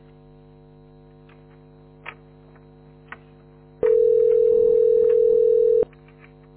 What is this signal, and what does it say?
Ringback tone (North America), 440 + 480 Hz.
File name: f80dc860-67e6-4459-8175